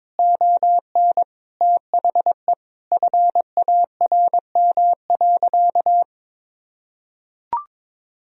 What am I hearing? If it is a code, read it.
Morse code, heard as a keyed tone, about 22 wpm, 700 Hz. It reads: ON THE FARM.
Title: Sentences from Top 1000 Words - Review Part 2 - 22wpm